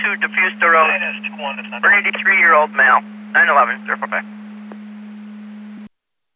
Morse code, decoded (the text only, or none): none